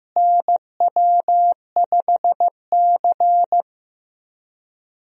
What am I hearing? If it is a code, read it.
Morse code, heard as a keyed tone, about 16 wpm, 700 Hz. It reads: NW5C
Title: USA Call Signs - 4 Characters - Repeated 3x - 15wpm